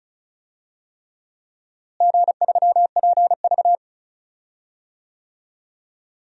Morse code, heard as a keyed tone, about 36 wpm, 700 Hz. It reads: G3PV
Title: DXCC-100 Call Signs - 4 Characters - Repeated 3x - 35wpm